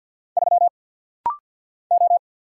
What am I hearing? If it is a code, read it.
Morse code, heard as a keyed tone, about 50 wpm, 700 Hz. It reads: 3 O
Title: Single Letter-Number Alt - Rapid-Fire 50wpm